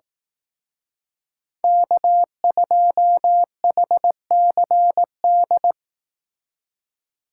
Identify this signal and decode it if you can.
Morse code, heard as a keyed tone, about 18 wpm, 700 Hz. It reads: K2HCD
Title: Speed Racing - USA 1x3 Call Signs 15wpm